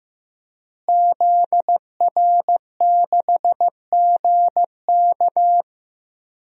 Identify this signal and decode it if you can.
Morse code, heard as a keyed tone, about 15 wpm, 700 Hz. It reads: ZR6GK
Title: DXCC-100 Call Signs - 5 Characters - Repeated 3x - 15wpm